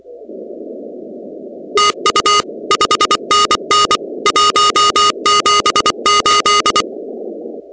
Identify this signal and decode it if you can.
Morse code, heard as a keyed tone, about 24 wpm, 1240 Hz, with noise. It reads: TU 5NN 178
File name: contests / 2019-cw-wpx-kt5j-k5pi- -2330 0-16